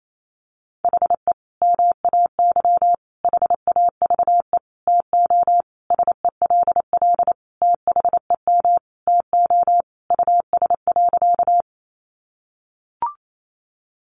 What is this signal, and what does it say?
Morse code, heard as a keyed tone, about 28 wpm, 700 Hz. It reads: HE MAY HAVE TO SELL THEM TO US.
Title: Sentences from Top 1100 Words 28wpm